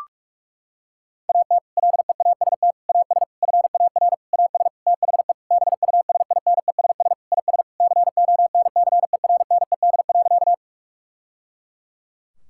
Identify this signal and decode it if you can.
Morse code, heard as a keyed tone, about 46 wpm, 700 Hz. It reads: AT LEAST AS FAR AS THE BUSINESS IS CONCERNED.
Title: Sentences from Top 1000 Words - Review Part 6 - 45wpm